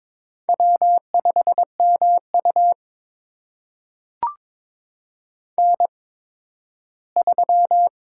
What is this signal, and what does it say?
Morse code, heard as a keyed tone, about 22 wpm, spaced as for 13 wpm, 700 Hz. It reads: W5MU N3
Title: USA Call Signs - 8x Character Spacing - 22wpm